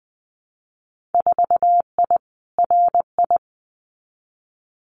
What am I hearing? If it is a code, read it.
Morse code, heard as a keyed tone, about 19 wpm, 700 Hz. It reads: 4I RI
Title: Winter Field Day Exchanges Encore 20wpm